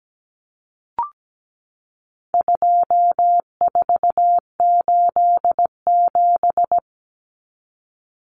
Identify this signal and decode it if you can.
Morse code, heard as a keyed tone, about 17 wpm, 700 Hz. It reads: 2487
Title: Numbers - 4-Digit 17wpm